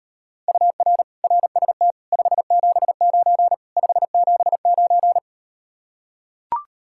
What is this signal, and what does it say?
Morse code, heard as a keyed tone, about 38 wpm, 700 Hz. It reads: UR RST 579 579